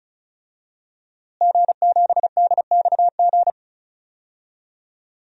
Morse code, heard as a keyed tone, about 36 wpm, 700 Hz. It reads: G7DXG
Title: DXCC-100 Call Signs - 5 Characters - Repeated 3x - 35wpm